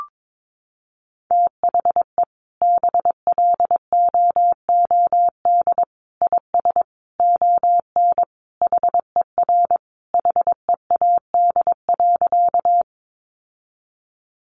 Morse code, heard as a keyed tone, about 22 wpm, 700 Hz. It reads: THE BLOOD IS ON HER HEAD.
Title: Sentences from Top 1100 Words 22wpm